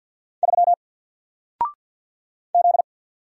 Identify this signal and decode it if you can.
Morse code, heard as a keyed tone, about 48 wpm, 700 Hz. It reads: 3 Z
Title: Single Letter-Number Alt 50wpm